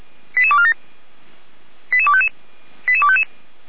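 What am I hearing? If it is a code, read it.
Five-tone selective call (ZVEI), heard as quick runs of five tones, tones of about 70 ms. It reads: 88247 88255 88255
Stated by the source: ZVEI five-tone call 88231 received by SDR, 455 channel.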